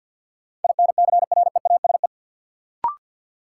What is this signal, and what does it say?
Morse code, heard as a keyed tone, about 52 wpm, 700 Hz. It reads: INCREASE
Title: Sentences from Top 1000 Words - Review Part 4 - New Word Drill - 50wpm